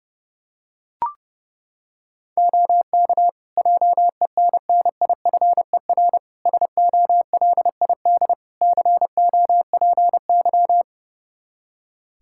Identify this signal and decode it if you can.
Morse code, heard as a keyed tone, about 29 wpm, 700 Hz. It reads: OK JENNIFER SOLID COPY
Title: Practice QSO - Exchange 3 - More Info - 30wpm